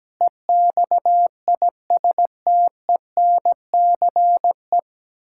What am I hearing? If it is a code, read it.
Morse code, heard as a keyed tone, about 17 wpm, 700 Hz. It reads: EXISTENCE
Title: Pangram Challenge v2 - 17wpm